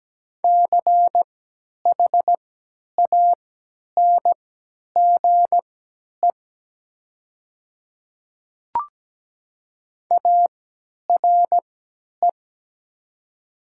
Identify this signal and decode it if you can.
Morse code, heard as a keyed tone, about 17 wpm, spaced as for 10 wpm, 700 Hz. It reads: CHANGE ARE
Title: 3x Character Spacing - Top 500 Words - 17wpm